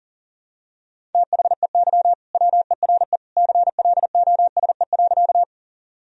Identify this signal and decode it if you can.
Morse code, heard as a keyed tone, about 40 wpm, 700 Hz. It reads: THEY WERE CLOSE.
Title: Sentences from Top 1000 Words - Review Part 1 - 40wpm